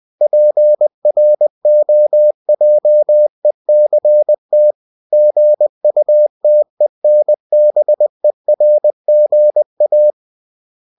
Morse code, heard as a keyed tone, about 20 wpm, 600 Hz. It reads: PROJECT GUTENBERGA
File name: cw-books / winnie-the-pooh / winnie-20wpm-page-0120